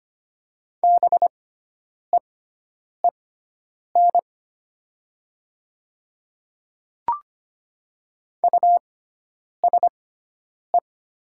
Morse code, heard as a keyed tone, about 25 wpm, spaced as for 9 wpm, 700 Hz. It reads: BEEN USE
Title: 6x Character Spacing - Top 100 Words - 25wpm